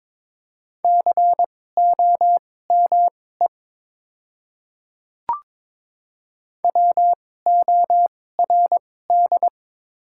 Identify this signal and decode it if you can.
Morse code, heard as a keyed tone, about 23 wpm, spaced as for 16 wpm, 690 Hz. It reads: COME WORD
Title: 2x Character Spacing - Top 200 Words - 22wpm